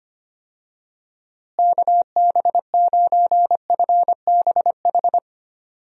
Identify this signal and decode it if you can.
Morse code, heard as a keyed tone, about 25 wpm, 700 Hz. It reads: KB9FBH
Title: USA Call Signs - 6 Characters - Repeated 2x - 25wpm